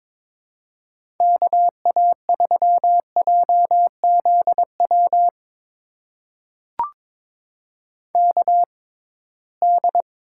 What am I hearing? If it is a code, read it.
Morse code, heard as a keyed tone, about 22 wpm, spaced as for 17 wpm, 700 Hz. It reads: KA3JZW KD
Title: USA Call Signs - 6 Characters - 6x Character Spacing - 22wpm